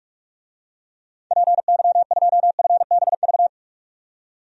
Morse code, heard as a keyed tone, about 42 wpm, 700 Hz. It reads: PY2FBV
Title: DXCC-100 Call Signs - 6 Characters - 45wpm